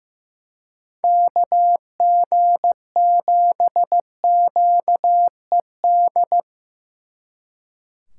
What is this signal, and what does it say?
Morse code, heard as a keyed tone, about 18 wpm, 700 Hz. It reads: KG7QED
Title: Speed Racing - USA 2x3 Call Signs 15wpm